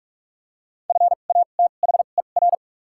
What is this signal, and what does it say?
Morse code, heard as a keyed tone, about 45 wpm, spaced as for 32 wpm, 700 Hz. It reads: FATHER
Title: 2x Character Spacing - Top 200 Words - 45wpm